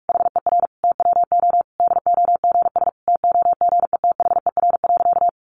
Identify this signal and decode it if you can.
Morse code, heard as a keyed tone, about 44 wpm, 700 Hz. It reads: HER TWO DOGS TOGETHER.
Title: Easy Phrases 45wpm